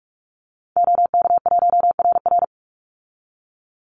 Morse code, heard as a keyed tone, about 44 wpm, 700 Hz. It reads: OK1RR
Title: Call Signs - 45wpm